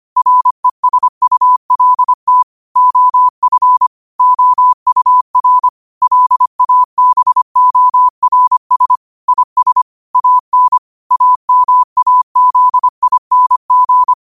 Morse code, heard as a keyed tone, about 25 wpm, 1000 Hz. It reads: RESULT OF OUR LABORS IS AN AMAZING